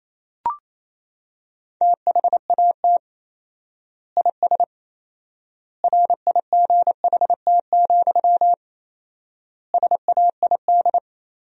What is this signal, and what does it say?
Morse code, heard as a keyed tone, about 28 wpm, 700 Hz. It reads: THAT IS RIGHT, SAID
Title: 4x Word-Spacing - Sentences from Top 200 Words - 28wpm